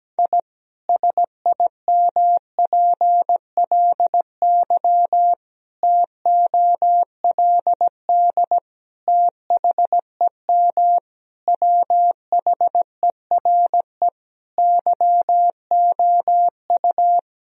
Morse code, heard as a keyed tone, about 17 wpm, 700 Hz. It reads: I SIMPLY TOLD THEM WHERE YOU
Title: Sentences from Top 1000 Words - Review Part 4 - 17wpm